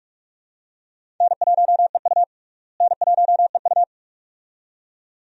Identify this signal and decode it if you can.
Morse code, heard as a keyed tone, about 46 wpm, 700 Hz. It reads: N1EU N1EU
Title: USA Call Signs - 4 Characters - Repeated 2x - 45wpm